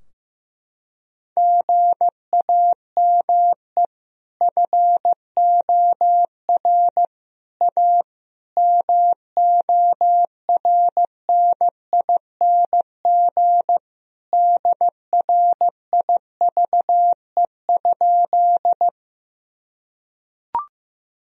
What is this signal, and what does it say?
Morse code, heard as a keyed tone, about 15 wpm, 700 Hz. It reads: GAME FOR A MORNING DRIVE?
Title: Sentences from Top 1000 Words - Review Part 3 - 15wpm